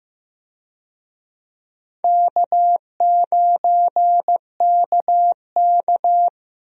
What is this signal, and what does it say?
Morse code, heard as a keyed tone, about 15 wpm, 700 Hz. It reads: K9KK
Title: USA Call Signs - 4 Characters - Repeated 3x - 15wpm